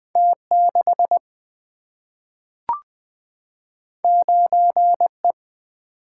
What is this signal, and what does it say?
Morse code, heard as a keyed tone, about 20 wpm, 700 Hz. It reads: T6 9E
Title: Single-Letter Single-Number 20wpm